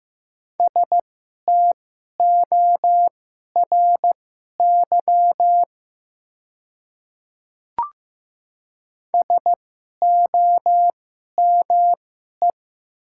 Morse code, heard as a keyed tone, about 15 wpm, spaced as for 11 wpm, 700 Hz. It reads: STORY SOME
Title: 2x Character Spacing - Top 300 Words - 15wpm